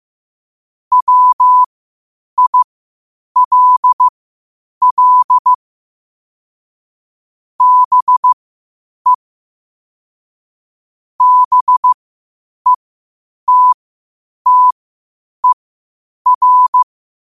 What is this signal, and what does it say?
Morse code, heard as a keyed tone, about 15 wpm, spaced as for 8 wpm, 1000 Hz. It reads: WILL BE BETTER
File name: qotdfiles / 8wpm